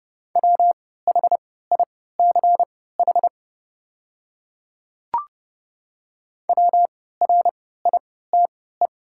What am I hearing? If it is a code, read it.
Morse code, heard as a keyed tone, about 31 wpm, spaced as for 17 wpm, 700 Hz. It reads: WHICH WRITE
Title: 3x Character Spacing - Top 100 Words - 30wpm